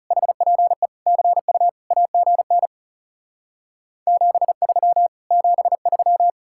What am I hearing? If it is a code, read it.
Morse code, heard as a keyed tone, about 40 wpm, 700 Hz. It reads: HPE CU AGN 73 73